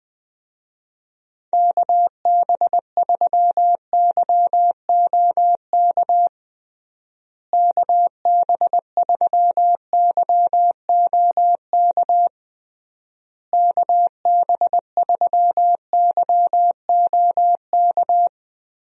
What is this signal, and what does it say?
Morse code, heard as a keyed tone, about 20 wpm, 700 Hz. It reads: KB3YOK KB3YOK KB3YOK
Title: USA Call Signs - 6 Characters - Repeated 3x - 20wpm